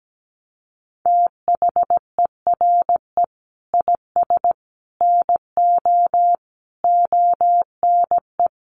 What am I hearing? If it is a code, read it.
Morse code, heard as a keyed tone, about 17 wpm, 700 Hz. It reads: THERE IS NO ONE
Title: Sentences from Top 500 Words 17wpm